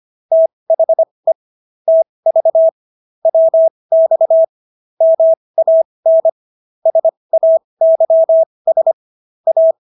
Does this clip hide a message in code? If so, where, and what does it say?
Yes: Morse code from 0.3 to 9.7 s, heard as a keyed tone, about 25 wpm, spaced as for 20 wpm, 650 Hz. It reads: THE TV WX MAN SAYS A